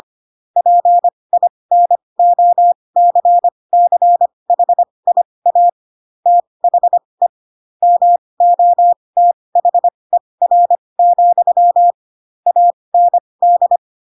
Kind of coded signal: Morse code